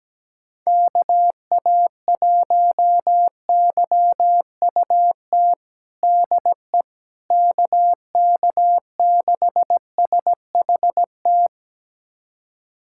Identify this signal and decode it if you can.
Morse code, heard as a keyed tone, about 17 wpm, 700 Hz. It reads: KA1YUT DE KK6SHT
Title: USA Callsign DE Callsign 17wpm